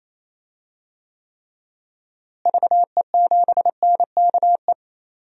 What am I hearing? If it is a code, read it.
Morse code, heard as a keyed tone, about 28 wpm, 700 Hz. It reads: VE7NKE
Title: DXCC-100 Call Signs - 6 Characters - Repeated 3x - 28wpm